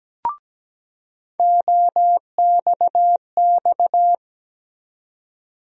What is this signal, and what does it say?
Morse code, heard as a keyed tone, about 17 wpm, 700 Hz. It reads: OXX
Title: USA Call Sign Postfixes - 3 Letters 17wpm